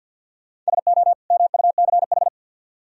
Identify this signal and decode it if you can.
Morse code, heard as a keyed tone, about 62 wpm, 700 Hz. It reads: SO MUCH
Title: Speed-Racing - Sets of 2 Words 50wpm